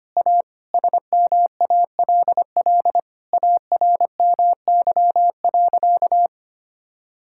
Morse code, heard as a keyed tone, about 25 wpm, 700 Hz. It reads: A SMALL ARMY.
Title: Sentences from Top 1000 Words - Review Part 6 - 25wpm